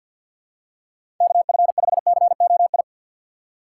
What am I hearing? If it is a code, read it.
Morse code, heard as a keyed tone, about 53 wpm, 700 Hz. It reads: KF5COI
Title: USA Call Signs - 6 Characters - 3x Character Spacing - 50wpm